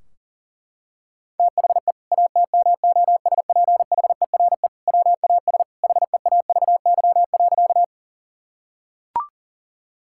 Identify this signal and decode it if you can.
Morse code, heard as a keyed tone, about 40 wpm, 700 Hz. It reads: THE ATMOSPHERE WAS HEAVY.
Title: Sentences from Top 1600 Words 40wpm